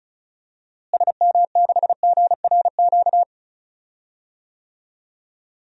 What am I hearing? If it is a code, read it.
Morse code, heard as a keyed tone, about 36 wpm, 700 Hz. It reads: SM6GRQ
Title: DXCC-100 Call Signs - 6 Characters - Repeated 2x - 35wpm